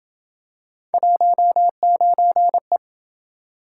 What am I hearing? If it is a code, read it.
Morse code, heard as a keyed tone, about 28 wpm, 700 Hz. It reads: 19E